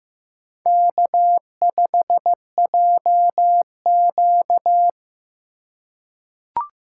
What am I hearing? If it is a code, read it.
Morse code, heard as a keyed tone, about 15 wpm, 700 Hz. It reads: K5JQ
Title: USA Call Signs - 4x Character Spacing - 15wpm